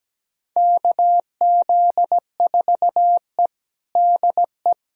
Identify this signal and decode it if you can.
Morse code, heard as a keyed tone, about 17 wpm, 700 Hz. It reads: KZ4E DE
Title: USA Callsign DE Callsign - 4 Characters - 17wpm